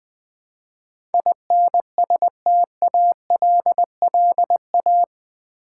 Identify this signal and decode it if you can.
Morse code, heard as a keyed tone, about 20 wpm, 700 Hz. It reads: INSTALLA
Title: Build A Word - 4 Letters At A Time - 12-Letter Words - 20wpm